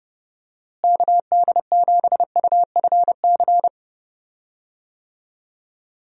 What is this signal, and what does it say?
Morse code, heard as a keyed tone, about 32 wpm, 700 Hz. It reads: KD7UFC
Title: USA Call Signs - 6 Characters - Repeated 3x - 30wpm